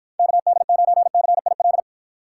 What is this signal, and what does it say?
Morse code, heard as a keyed tone, about 52 wpm, 700 Hz. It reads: KD9CIL